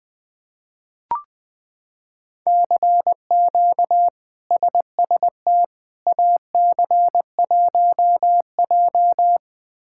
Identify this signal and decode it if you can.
Morse code, heard as a keyed tone, about 21 wpm, 700 Hz. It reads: CQ SST AC1J
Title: K1USN SST QSO Practice - 20wpm